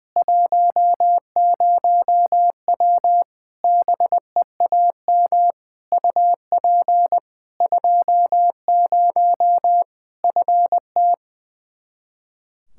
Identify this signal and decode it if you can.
Morse code, heard as a keyed tone, about 21 wpm, 700 Hz. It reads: 10W BEAM UP 20 FT
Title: Practice QSO - Exchange 2 - Info - 20wpm